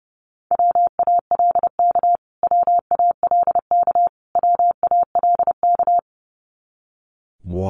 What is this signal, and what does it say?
Morse code, heard as a keyed tone, about 30 wpm, 700 Hz. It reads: WALK WALK WALK
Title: Top 300 Words Repeated 30wpm